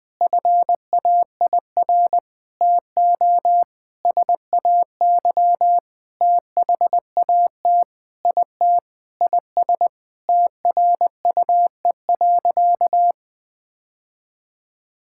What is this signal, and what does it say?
Morse code, heard as a keyed tone, about 21 wpm, 700 Hz. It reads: FAIR TO SAY THAT IT IS TRUE.
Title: Sentences with 8-Word Phrases - 20wpm